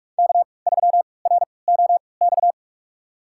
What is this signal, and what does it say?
Morse code, heard as a keyed tone, about 46 wpm, spaced as for 26 wpm, 700 Hz. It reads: K3ROX
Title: USA Call Signs - 5 Characters - 3x Character Spacing - 45wpm